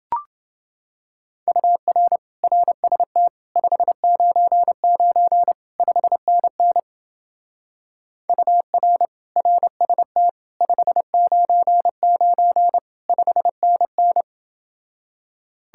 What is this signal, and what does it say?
Morse code, heard as a keyed tone, about 30 wpm, 700 Hz. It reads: UR RST 599 5NN UR RST 599 5NN